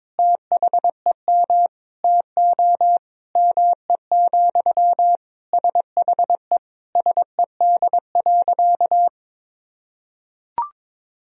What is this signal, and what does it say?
Morse code, heard as a keyed tone, about 22 wpm, 700 Hz. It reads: THEM TO ME, SHE SED.
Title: Sentences from Top 500 Words and Abbreviations 22wpm